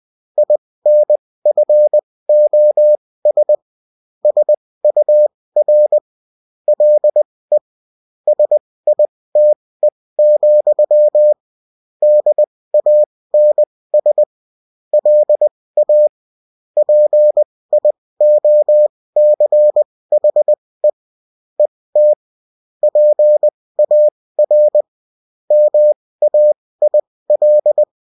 Morse code, heard as a keyed tone, about 20 wpm, spaced as for 16 wpm, 600 Hz. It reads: INFOS SUR LE SITE, DANS LA PIOCHE ET PAR MAIL